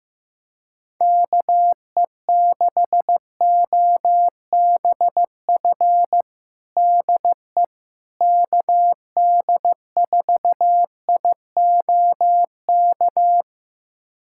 Morse code, heard as a keyed tone, about 15 wpm, 700 Hz. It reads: KE6OBF DE KD4IOK
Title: USA Callsign DE Callsign 15wpm